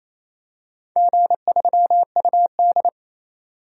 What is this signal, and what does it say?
Morse code, heard as a keyed tone, about 28 wpm, 700 Hz. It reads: G3UD
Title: DXCC-100 Call Signs - 4 Characters - 28wpm